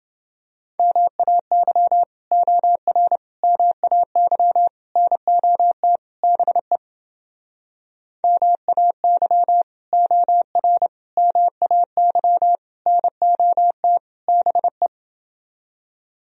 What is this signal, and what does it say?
Morse code, heard as a keyed tone, about 30 wpm, 700 Hz. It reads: MAY OR MAY NOT BE MAY OR MAY NOT BE